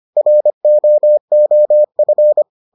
Morse code, heard as a keyed tone, about 25 wpm, 600 Hz. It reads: ROOF